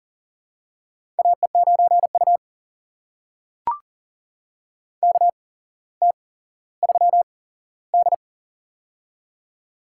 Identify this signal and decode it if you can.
Morse code, heard as a keyed tone, about 40 wpm, spaced as for 17 wpm, 700 Hz. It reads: AE9U KT3D
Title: USA Call Signs - 4 Characters - 8x Character Spacing - 40wpm